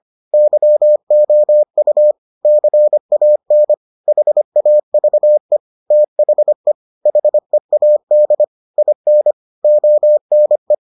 Morse code, heard as a keyed tone, about 25 wpm, 600 Hz. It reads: YOU CAN HAVE THE HEAD IN ONE